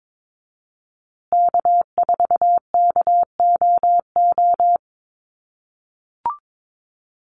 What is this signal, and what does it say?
Morse code, heard as a keyed tone, about 22 wpm, 700 Hz. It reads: K4KOO
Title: USA Call Signs Repeated 2x 22wpm